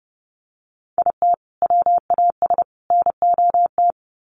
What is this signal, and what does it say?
Morse code, heard as a keyed tone, about 30 wpm, 700 Hz. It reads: IT WAS NOT
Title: Sets of 3 Words 30wpm